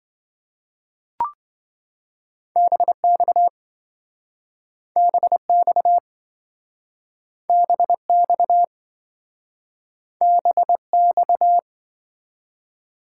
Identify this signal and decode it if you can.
Morse code, heard as a keyed tone, about 27 wpm, 700 Hz. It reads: BX BX BX BX